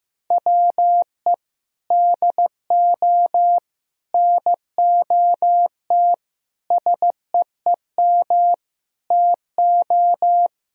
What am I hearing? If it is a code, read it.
Morse code, heard as a keyed tone, about 15 wpm, 700 Hz. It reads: WE DO NOT SEEM TO
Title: Sentences from Top 1000 Words - Review Part 1 - 15wpm